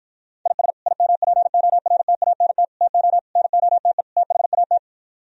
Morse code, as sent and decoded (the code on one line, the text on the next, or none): .. ... / .. -- .--. --- .-. - .- -. - / - --- / -. --- - . / - .... .- -
IS IMPORTANT TO NOTE THAT